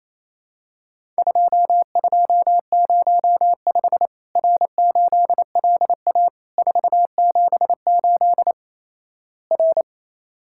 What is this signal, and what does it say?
Morse code, heard as a keyed tone, about 28 wpm, 700 Hz. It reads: 2205 R8LA 478 R
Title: Worked All Europe DX Contest - QTC Exchange - 28wpm